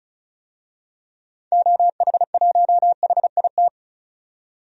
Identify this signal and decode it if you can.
Morse code, heard as a keyed tone, about 36 wpm, 690 Hz. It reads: OH1HIT
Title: DXCC-100 Call Signs - 6 Characters - 35wpm